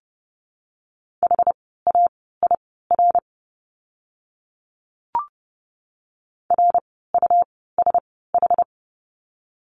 Morse code, heard as a keyed tone, about 30 wpm, spaced as for 17 wpm, 700 Hz. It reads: HAIR RUSH
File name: Lesson 17f - Letter u - Words 30wpm